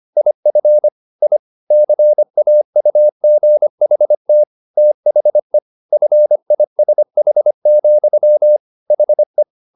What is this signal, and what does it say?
Morse code, heard as a keyed tone, about 25 wpm, 600 Hz. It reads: IF I CAUGHT THE FISH, HE